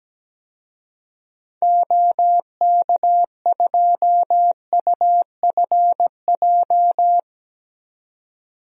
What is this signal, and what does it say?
Morse code, heard as a keyed tone, about 17 wpm, 700 Hz. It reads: OK2UFJ
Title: DXCC-100 Call Signs - 6 Characters - 17wpm